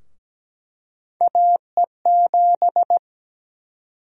Morse code, heard as a keyed tone, about 17 wpm, 700 Hz. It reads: AE7
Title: USA Call Sign Prefixes - 17wpm